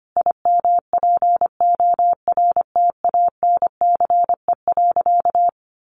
Morse code, heard as a keyed tone, about 25 wpm, 700 Hz. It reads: IMPORTANCE.
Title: Sentences from Top 1200 Words 25wpm